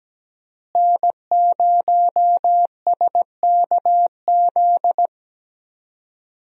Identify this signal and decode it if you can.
Morse code, heard as a keyed tone, about 17 wpm, 700 Hz. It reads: N0SKZ
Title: USA Call Signs - 5 Characters - Repeated 3x - 17wpm